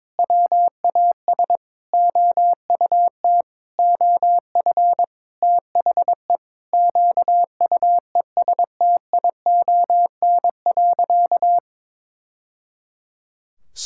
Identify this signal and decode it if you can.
Morse code, heard as a keyed tone, about 21 wpm, 700 Hz. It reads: WAS OUT OF THE QUESTION.
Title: Sentences from Top 1000 Words - Review Part 5 - 22wpm